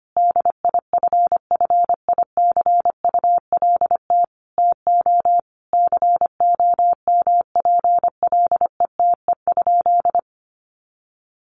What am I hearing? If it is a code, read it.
Morse code, heard as a keyed tone, about 25 wpm, 700 Hz. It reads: DIFFICULT TO COMPLETE?
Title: Sentences from Top 1500 Words 25wpm